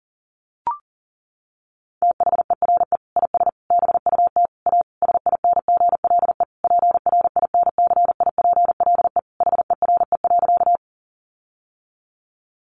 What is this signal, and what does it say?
Morse code, heard as a keyed tone, about 40 wpm, 700 Hz. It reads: THERE IS BUT A SINGLE PRINCIPLE HERE.
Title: Sentences from Top 1400 Words 40wpm